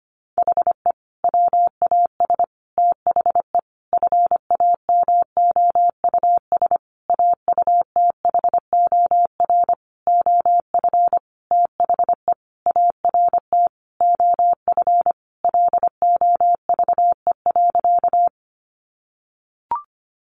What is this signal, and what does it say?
Morse code, heard as a keyed tone, about 25 wpm, 700 Hz. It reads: HE WAS THE FAMOUS AUTHOR OF THE ART OF LOVE.
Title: Sentences from Top 1500 Words 25wpm